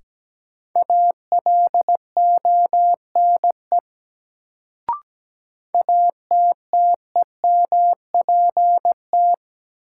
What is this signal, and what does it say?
Morse code, heard as a keyed tone, about 17 wpm, 700 Hz. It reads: ALONE ATTEMPT
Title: Sentences from Top 1000 Words - New Word Drill - 17wpm